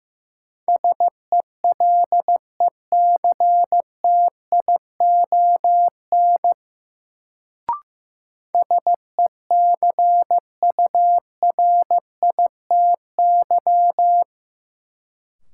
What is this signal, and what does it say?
Morse code, heard as a keyed tone, about 15 wpm, 700 Hz. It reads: SELECTION SECURITY
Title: Sentences from Top 1400 Words - New Word Drill Alt - 15wpm